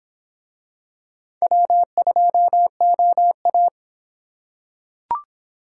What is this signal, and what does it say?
Morse code, heard as a keyed tone, about 26 wpm, 700 Hz. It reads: W2OA